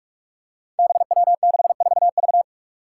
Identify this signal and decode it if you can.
Morse code, heard as a keyed tone, about 44 wpm, 700 Hz. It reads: BW64V
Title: Five Challenging Letter-Number Combinations 45wpm